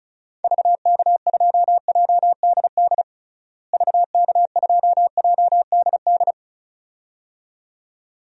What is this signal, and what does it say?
Morse code, heard as a keyed tone, about 34 wpm, 700 Hz. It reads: VK2JDD VK2JDD
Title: DXCC-100 Call Signs - 6 Characters - Repeated 2x - 35wpm